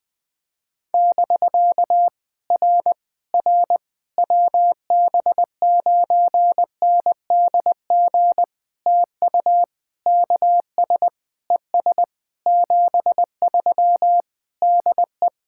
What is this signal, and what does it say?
Morse code, heard as a keyed tone, about 20 wpm, 700 Hz. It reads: <BK> R R WB9NDG TU KS ES 73 DE
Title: POTA QSO Practice - 20wpm